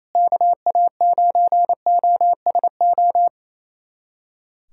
Morse code, heard as a keyed tone, about 28 wpm, 700 Hz. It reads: KA9OSO
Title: USA Callsign DE Callsign - 6 Characters - 28wpm